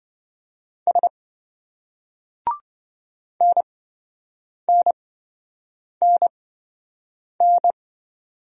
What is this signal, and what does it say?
Morse code, heard as a keyed tone, about 27 wpm, spaced as for 7 wpm, 700 Hz. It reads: S NNNN